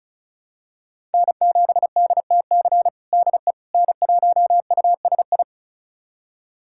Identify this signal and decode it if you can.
Morse code, heard as a keyed tone, about 36 wpm, 700 Hz. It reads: N7DTC DE N1USI
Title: USA Callsign DE Callsign - 5 Characters - 35wpm